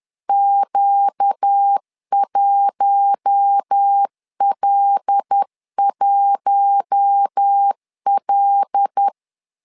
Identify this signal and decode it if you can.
Morse code, heard as a keyed tone, about 10 wpm, 800 Hz. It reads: Q1L1L
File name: LEZIONI / 5 LBVC / 5 L 10wpm 2259